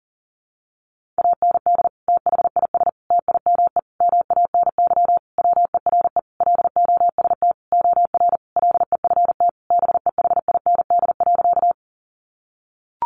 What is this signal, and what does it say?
Morse code, heard as a keyed tone, about 40 wpm, 700 Hz. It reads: AND THIS TIME MANY WERE LOST OR LEFT BEHIND.
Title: Sentences from Top 1400 Words 40wpm